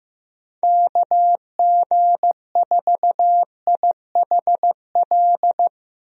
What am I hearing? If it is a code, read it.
Morse code, heard as a keyed tone, about 15 wpm, 700 Hz. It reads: KG4IHL
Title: USA Call Signs - 6 Characters - Repeated 2x - 15wpm